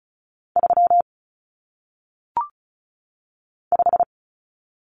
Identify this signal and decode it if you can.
Morse code, heard as a keyed tone, about 36 wpm, 700 Hz. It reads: 3 5
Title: Common Cut Numbers - 1-Digit 35wpm